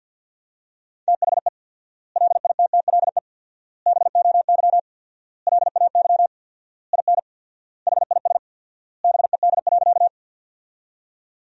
Morse code, heard as a keyed tone, about 51 wpm, 700 Hz. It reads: THE LITTLE BOY LAY IN HIS BED.
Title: 4x Word-Spacing - Sentences from Top 500 Words - 50wpm